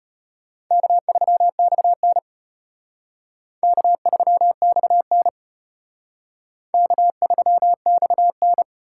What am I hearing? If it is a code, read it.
Morse code, heard as a keyed tone, about 34 wpm, 700 Hz. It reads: K3XN K3XN K3XN